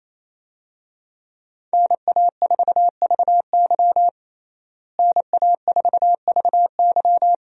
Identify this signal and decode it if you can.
Morse code, heard as a keyed tone, about 28 wpm, 700 Hz. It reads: NA4VY NA4VY
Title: USA Call Signs - 5 Characters - Repeated 2x - 28wpm